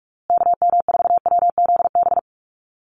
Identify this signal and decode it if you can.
Morse code, heard as a keyed tone, about 44 wpm, 700 Hz. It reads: KM4WZB
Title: USA Call Signs Repeated 3x 45wpm